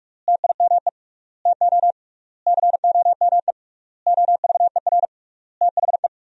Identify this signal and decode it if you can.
Morse code, heard as a keyed tone, about 46 wpm, 700 Hz. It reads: TIME TO COME OVER THE
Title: 3x Word-Spacing - Sentences from Top 300 Words - 45wpm